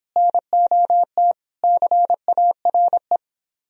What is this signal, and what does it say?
Morse code, heard as a keyed tone, about 24 wpm, 700 Hz. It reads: NOT CARE